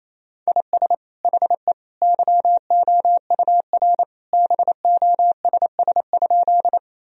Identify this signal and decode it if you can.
Morse code, heard as a keyed tone, about 28 wpm, 700 Hz. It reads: IS HE YOUR BOSS?
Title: Sentences from Top 1000 Words - Review Part 4 - 28wpm